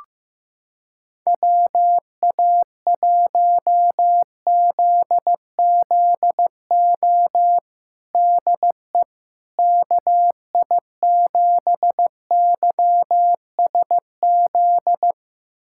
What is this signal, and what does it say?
Morse code, heard as a keyed tone, about 15 wpm, 700 Hz. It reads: WA1ZZO DE KI7YSZ
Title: USA Callsign DE Callsign - 6 Characters - 15wpm